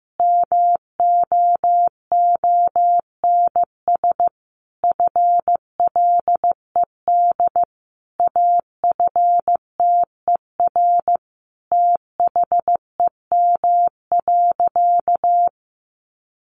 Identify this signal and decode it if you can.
Morse code, heard as a keyed tone, about 15 wpm, 700 Hz. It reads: MOONS FLED AFTER THEM.
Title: A Sound Of Thunder 15wpm